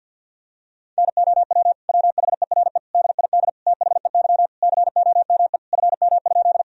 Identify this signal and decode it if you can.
Morse code, heard as a keyed tone, about 50 wpm, 700 Hz. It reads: NOW WHERE DID THEY COME FM?
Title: Sentences from Top 200 Words and Abbreviations 50wpm